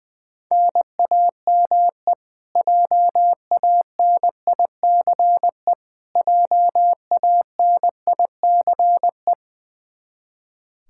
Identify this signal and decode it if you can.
Morse code, heard as a keyed tone, about 21 wpm, 700 Hz. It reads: NAME JANICE JANICE
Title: Practice QSO - Exchange 1 - Beginning - 20wpm